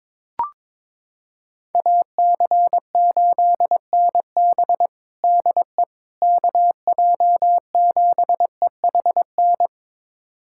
Morse code, heard as a keyed tone, about 22 wpm, 700 Hz. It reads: AC8NB DE KJ7EHN
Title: USA Callsign DE Callsign 22wpm